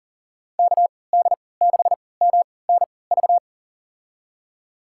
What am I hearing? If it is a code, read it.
Morse code, heard as a keyed tone, about 40 wpm, spaced as for 23 wpm, 700 Hz. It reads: KD6MNV
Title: USA Call Signs - 6 Characters - 3x Character Spacing - 40wpm